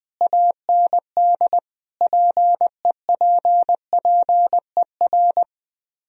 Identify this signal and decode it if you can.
Morse code, heard as a keyed tone, about 20 wpm, 700 Hz. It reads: AND PEPPER